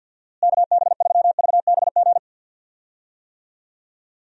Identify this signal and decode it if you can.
Morse code, heard as a keyed tone, about 50 wpm, 700 Hz. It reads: KB3VBG
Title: USA Call Signs - 6 Characters - Repeated 3x - 50wpm